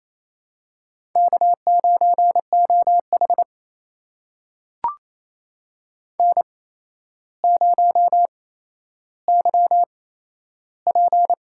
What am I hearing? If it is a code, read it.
Morse code, heard as a keyed tone, about 28 wpm, spaced as for 12 wpm, 700 Hz. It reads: K9OH N0YP
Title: USA Call Signs - 4 Characters - 8x Character Spacing - 28wpm